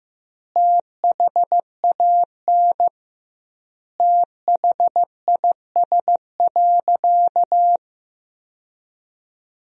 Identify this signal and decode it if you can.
Morse code, heard as a keyed tone, about 16 wpm, 700 Hz. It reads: THAN THIS.
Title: 2x Word-Spacing - Sentences from Top 100 Words - 15wpm